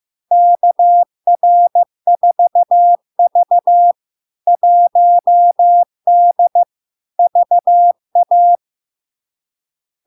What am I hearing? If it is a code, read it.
Morse code, heard as a keyed tone, about 15 wpm, 700 Hz. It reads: KR4V 1D VA
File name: LARK-FD-2022-Log-in-CW-15wpm-700-Hz